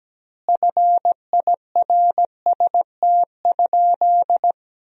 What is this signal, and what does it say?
Morse code, heard as a keyed tone, about 17 wpm, 700 Hz. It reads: FIRST?
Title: 4x Word-Spacing - Sentences from Top 100 Words - 17wpm